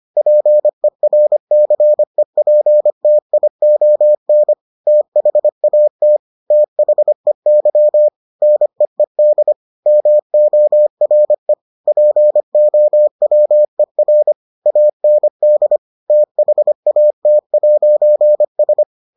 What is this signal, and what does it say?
Morse code, heard as a keyed tone, about 25 wpm, 600 Hz. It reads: PERCEPTION THAT THEY NEED MORE POWER AND THAT'S